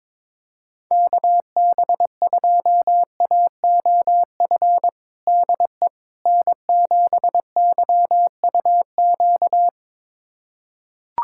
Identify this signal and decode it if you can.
Morse code, heard as a keyed tone, about 22 wpm, 700 Hz. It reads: KB2AOF DE N7YUQ
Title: USA Callsign DE Callsign 22wpm